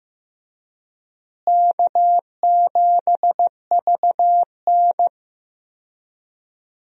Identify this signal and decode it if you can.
Morse code, heard as a keyed tone, about 15 wpm, 700 Hz. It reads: K7VN
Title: USA Call Signs - 4 Characters - Repeated 3x - 15wpm